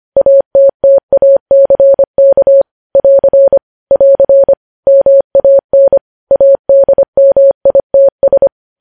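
Morse code, heard as a keyed tone, about 25 wpm, 550 Hz. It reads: ATTACK <AR> <AR> MAN ADMITS
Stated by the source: Morse practice of news headlines at 25wpm.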